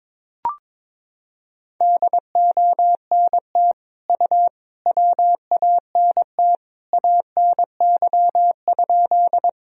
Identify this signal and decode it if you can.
Morse code, heard as a keyed tone, about 22 wpm, 700 Hz. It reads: DONT U WANT ANY?
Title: Sentences from Top 300 Words and Abbreviations 22wpm